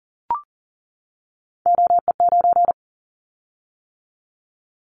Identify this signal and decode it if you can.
Morse code, heard as a keyed tone, about 41 wpm, 700 Hz. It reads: OE9
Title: Callsign Prefixes for Top 100 DXCC - 40wpm